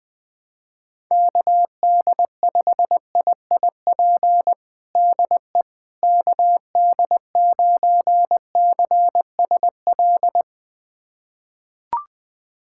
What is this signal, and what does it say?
Morse code, heard as a keyed tone, about 21 wpm, 700 Hz. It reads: KD5IIP DE KD9CSL
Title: USA Callsign DE Callsign - 6 Characters - 20wpm